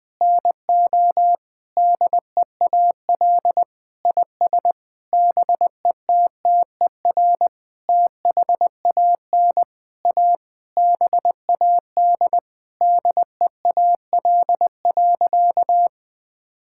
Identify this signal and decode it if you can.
Morse code, heard as a keyed tone, about 21 wpm, 700 Hz. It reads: NO DEAL IS BETTER THAN A BAD DEAL.
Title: Sentences with 9-Word Phrases - Spoken Beginning - 20wpm